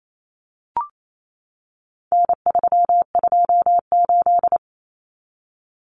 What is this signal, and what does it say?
Morse code, heard as a keyed tone, about 28 wpm, 700 Hz. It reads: N328
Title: Common Cut Numbers - 4-Digit 28wpm